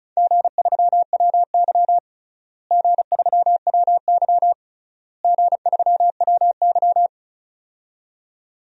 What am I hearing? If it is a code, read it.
Morse code, heard as a keyed tone, about 36 wpm, 700 Hz. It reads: G3WY G3WY G3WY
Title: DXCC-100 Call Signs - 4 Characters - Repeated 3x - 35wpm